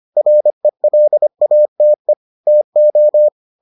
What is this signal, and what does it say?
Morse code, heard as a keyed tone, about 25 wpm, 600 Hz. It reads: RELATE TO